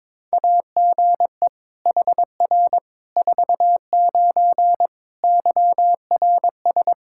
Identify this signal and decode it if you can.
Morse code, heard as a keyed tone, about 22 wpm, 700 Hz. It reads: AGE HR 49 YRS
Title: K1USN SST QSO Practice 22wpm